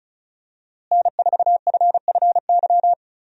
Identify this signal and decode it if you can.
Morse code, heard as a keyed tone, about 36 wpm, 700 Hz. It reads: N4FFY
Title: USA Call Signs - 5 Characters - Repeated 3x - 35wpm